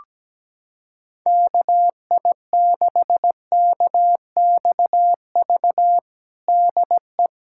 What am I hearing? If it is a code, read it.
Morse code, heard as a keyed tone, about 17 wpm, 700 Hz. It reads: KI6KXV DE
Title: USA Callsign DE Callsign 17wpm